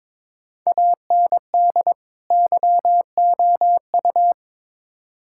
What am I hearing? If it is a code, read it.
Morse code, heard as a keyed tone, about 22 wpm, 700 Hz. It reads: AND YOU
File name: Build A Phrase - 2-Word Phrases - 22wpm